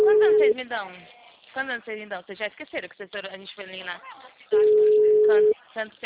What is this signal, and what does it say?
Ringback tone (Europe, 425 Hz), with speech.
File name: out- -2007- - - .84982